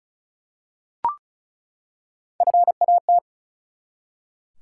Morse code, heard as a keyed tone, about 36 wpm, 700 Hz. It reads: FAT
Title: Three Letter Words - 35wpm